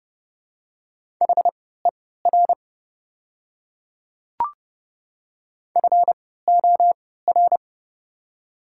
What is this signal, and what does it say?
Morse code, heard as a keyed tone, about 30 wpm, spaced as for 17 wpm, 700 Hz. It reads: HER FOR
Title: 3x Character Spacing - Top 100 Words - 30wpm